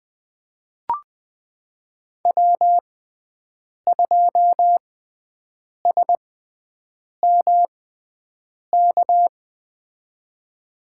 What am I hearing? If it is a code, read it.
Morse code, heard as a keyed tone, about 20 wpm, spaced as for 7 wpm, 700 Hz. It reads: W2SMK
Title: USA Call Signs - 5 Characters - 6x Character Spacing - 20wpm